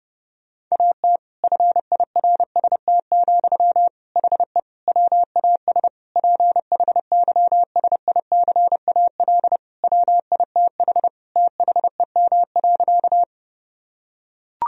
Morse code, heard as a keyed tone, about 30 wpm, 700 Hz. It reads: AT FIRST, HE WAS PHYSICAL WITH THEM.
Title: Sentences from Top 1000 Words - Review Part 4 - 30wpm